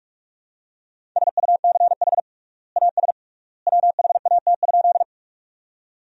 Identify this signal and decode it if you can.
Morse code, heard as a keyed tone, about 46 wpm, 700 Hz. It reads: SUCH AS WHAT?
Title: 3x Word-Spacing - Sentences from Top 200 Words - 45wpm